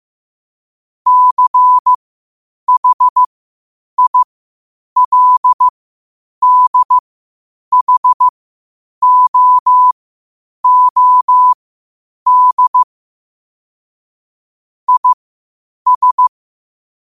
Morse code, heard as a keyed tone, about 15 wpm, spaced as for 8 wpm, 1000 Hz. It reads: CHILDHOOD IS
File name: qotdfiles / 8wpm